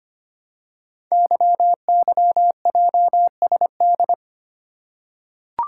Morse code, heard as a keyed tone, about 25 wpm, 700 Hz. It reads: YYJSD
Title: Code Groups - 5 Letters - 25wpm